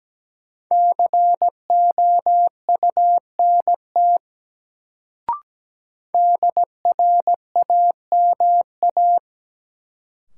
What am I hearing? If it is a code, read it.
Morse code, heard as a keyed tone, about 17 wpm, 700 Hz. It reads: COUNT DRAMA
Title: Sentences from Top 1500 Words - New Word Drill - 17wpm